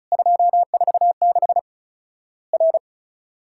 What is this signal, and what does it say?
Morse code, heard as a keyed tone, about 36 wpm, 700 Hz. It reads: 246 R
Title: Worked All Europe DX Contest - QTC Exchange - 35wpm